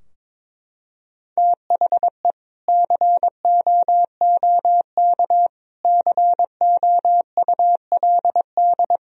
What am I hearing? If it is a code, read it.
Morse code, heard as a keyed tone, about 22 wpm, 700 Hz. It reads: THE COOK COULD
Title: Sentences from Top 1000 Words - Review Part 4 - 22wpm